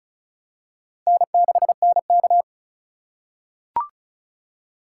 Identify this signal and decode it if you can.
Morse code, heard as a keyed tone, about 34 wpm, 700 Hz. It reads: N6NK
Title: USA Call Signs - 4 Characters - 2x Character Spacing - 35wpm